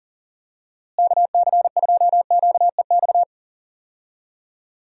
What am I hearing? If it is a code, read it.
Morse code, heard as a keyed tone, about 40 wpm, 700 Hz. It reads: KC2QEX
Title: USA Call Signs - 6 Characters - Repeated 2x - 40wpm